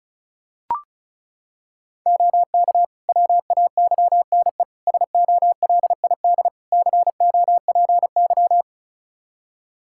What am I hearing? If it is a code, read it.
Morse code, heard as a keyed tone, about 36 wpm, 700 Hz. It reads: OK WAYNE SOLID COPY
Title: Practice QSO - Exchange 3 - More Info - 35wpm